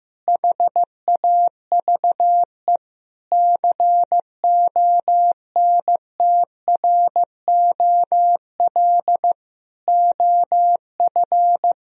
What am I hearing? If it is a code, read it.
Morse code, heard as a keyed tone, about 15 wpm, 700 Hz. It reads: HAVE CONTROL OF
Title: Sentences from Top 1000 Words - Review Part 3 - 15wpm